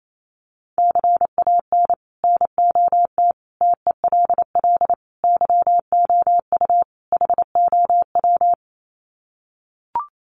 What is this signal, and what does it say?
Morse code, heard as a keyed tone, about 28 wpm, 700 Hz. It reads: CAN NOT TELL YOU HOW
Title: Sets of 5 Words Encore2 28wpm